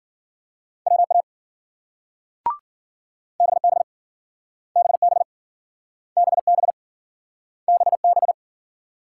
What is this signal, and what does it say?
Morse code, heard as a keyed tone, about 52 wpm, 700 Hz. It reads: WA BB BB BB BB